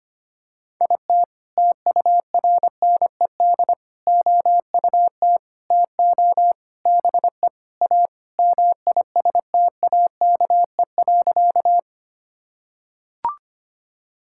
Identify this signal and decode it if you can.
Morse code, heard as a keyed tone, about 25 wpm, 700 Hz. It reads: IT TURNED OUT TO BE A MISTAKE.
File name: practice / 25wpm / Sentences with 4-Word Phrases - 25wpm